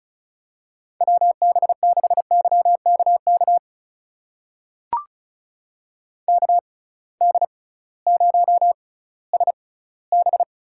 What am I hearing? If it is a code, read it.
Morse code, heard as a keyed tone, about 36 wpm, spaced as for 19 wpm, 700 Hz. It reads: WB6YKK KD0SB
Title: USA Call Signs - 6 Characters - 6x Character Spacing - 35wpm